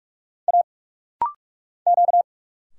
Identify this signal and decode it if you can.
Morse code, heard as a keyed tone, about 46 wpm, 700 Hz. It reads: A Q
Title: Single Letter-Number Alt - Rapid-Fire 45wpm